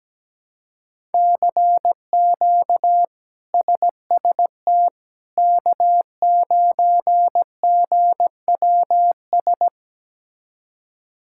Morse code, heard as a keyed tone, about 17 wpm, 700 Hz. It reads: CQ SST K9GWS
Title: K1USN SST QSO Practice - 17wpm